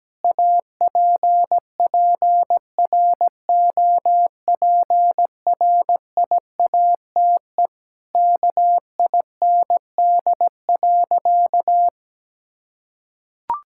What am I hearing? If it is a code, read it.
Morse code, heard as a keyed tone, about 17 wpm, 700 Hz. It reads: APPROPRIATE KIND.
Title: Sentences from Top 1000 Words - Review Part 5 - 17wpm